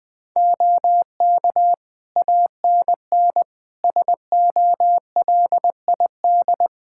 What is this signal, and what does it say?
Morse code, heard as a keyed tone, about 21 wpm, 700 Hz. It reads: OK ANN SOLID
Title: K1USN SST QSO Practice 20wpm